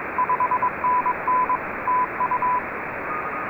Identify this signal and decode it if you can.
Morse code, heard as a keyed tone, about 22 wpm, 1010 Hz, with radio noise. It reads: 5NN TU
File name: RS44-VU2LBW